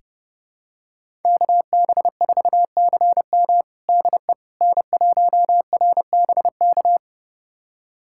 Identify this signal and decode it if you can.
Morse code, heard as a keyed tone, about 30 wpm, 700 Hz. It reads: KB4CM DE N1RBK
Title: USA Callsign DE Callsign - 5 Characters - 30wpm